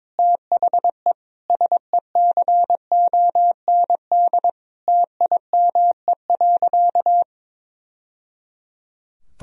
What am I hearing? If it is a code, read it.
Morse code, heard as a keyed tone, about 22 wpm, 700 Hz. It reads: THE SECOND TIME.
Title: Sentences from Top 1000 Words - Review Part 1 - 22wpm